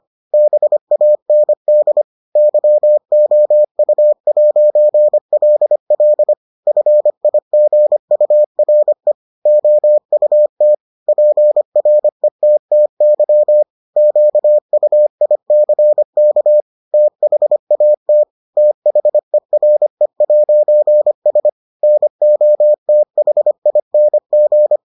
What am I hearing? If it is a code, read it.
Morse code, heard as a keyed tone, about 25 wpm, 600 Hz. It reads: BAND YOU'LL FIGURE OUT PRETTY QUICK THAT THERE'S NOTHING